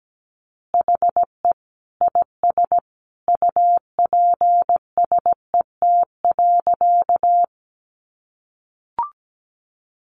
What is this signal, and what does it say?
Morse code, heard as a keyed tone, about 17 wpm, 700 Hz. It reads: HE IS UPSET.
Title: Additional Easy Phrases 17wpm